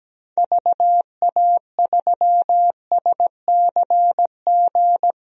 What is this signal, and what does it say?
Morse code, heard as a keyed tone, about 17 wpm, 700 Hz. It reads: VA3SCG
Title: DXCC-100 Call Signs - 6 Characters - Repeated 2x - 17wpm